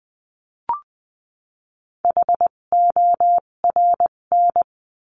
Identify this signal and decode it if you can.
Morse code, heard as a keyed tone, about 20 wpm, spaced as for 17 wpm, 700 Hz. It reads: HORN
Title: 257 Lesson 42b - Words 14-20wpm